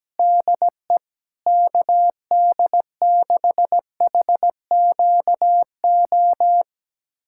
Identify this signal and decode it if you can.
Morse code, heard as a keyed tone, about 17 wpm, 700 Hz. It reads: DE KD6HQO
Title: USA Callsign DE Callsign 17wpm